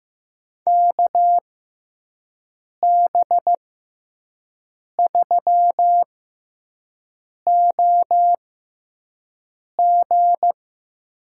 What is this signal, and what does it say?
Morse code, heard as a keyed tone, about 15 wpm, spaced as for 5 wpm, 700 Hz. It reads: KB3OG
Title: USA Call Signs - 6 Characters - 6x Character Spacing - 15wpm